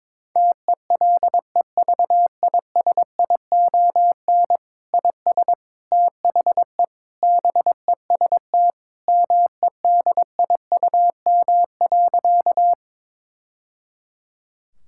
Morse code, heard as a keyed tone, about 22 wpm, 700 Hz. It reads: TELEVISION IS THE BEST MEDIUM.
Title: Sentences from Top 1000 Words - Review Part 6 - 22wpm